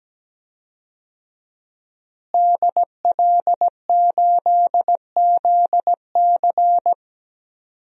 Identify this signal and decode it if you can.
Morse code, heard as a keyed tone, about 17 wpm, 700 Hz. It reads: DL8ZC
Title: DXCC-100 Call Signs - 5 Characters - Repeated 3x - 17wpm